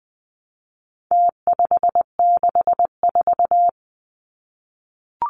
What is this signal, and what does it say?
Morse code, heard as a keyed tone, about 20 wpm, 700 Hz. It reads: T564
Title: Numbers - Contest Style - 4-Digit 20wpm